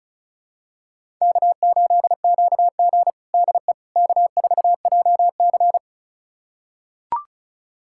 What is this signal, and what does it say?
Morse code, heard as a keyed tone, about 36 wpm, 700 Hz. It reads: K8QG DE K4JC
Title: USA Callsign DE Callsign - 4 Characters - 35wpm